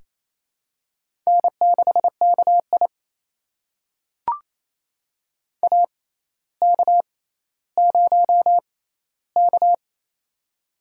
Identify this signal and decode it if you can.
Morse code, heard as a keyed tone, about 28 wpm, spaced as for 14 wpm, 700 Hz. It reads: N6KI AK0K
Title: USA Call Signs - 4 Characters - 6x Character Spacing - 28wpm